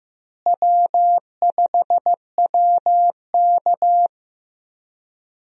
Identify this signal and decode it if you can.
Morse code, heard as a keyed tone, about 16 wpm, 700 Hz. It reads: W5WK
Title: USA Call Signs - 4 Characters - Repeated 2x - 15wpm